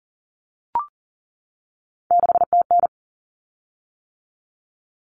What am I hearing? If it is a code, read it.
Morse code, heard as a keyed tone, about 38 wpm, 700 Hz. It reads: 6TN
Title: Common Cut Numbers - 3-Digit 40wpm